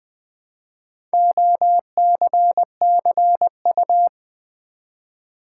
Morse code, heard as a keyed tone, about 20 wpm, 700 Hz. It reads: OCCU
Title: Build A Word - 1 Letter At A Time - 5-Letter Words - 20wpm